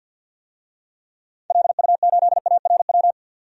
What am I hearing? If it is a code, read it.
Morse code, heard as a keyed tone, about 47 wpm, 700 Hz. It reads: LU8ARW
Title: DXCC-100 Call Signs - 6 Characters - 50wpm